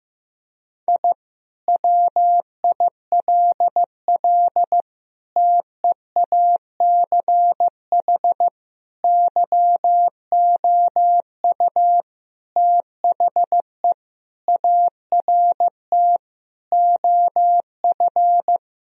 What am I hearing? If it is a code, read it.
Morse code, heard as a keyed tone, about 15 wpm, 700 Hz. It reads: I WILL TEACH YOU THE ART OF
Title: Sentences from Top 1000 Words - Review Part 3 - 15wpm